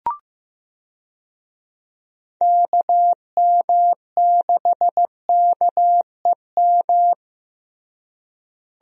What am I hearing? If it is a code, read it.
Morse code, heard as a keyed tone, about 15 wpm, 700 Hz. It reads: KM6KEM
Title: USA Call Signs - 6 Characters - Repeated 2x - 15wpm